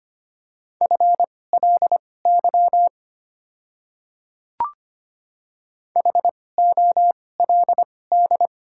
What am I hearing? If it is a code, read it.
Morse code, heard as a keyed tone, about 25 wpm, spaced as for 18 wpm, 690 Hz. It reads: FLY HOLD
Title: 2x Character Spacing - Top 500 Words - 25wpm